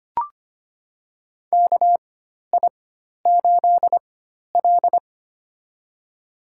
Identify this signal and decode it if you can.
Morse code, heard as a keyed tone, about 25 wpm, spaced as for 12 wpm, 700 Hz. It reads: KI8L
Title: USA Call Signs - 4 Characters - 4x Character Spacing - 25wpm